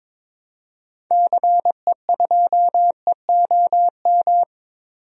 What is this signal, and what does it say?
Morse code, heard as a keyed tone, about 22 wpm, 700 Hz. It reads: CE2EOM
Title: DXCC-100 Call Signs - 6 Characters - 22wpm